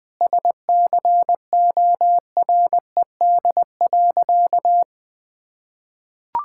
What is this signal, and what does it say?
Morse code, heard as a keyed tone, about 19 wpm, 700 Hz. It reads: SCORED.
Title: Sentences with 6-Word Phrases - 20wpm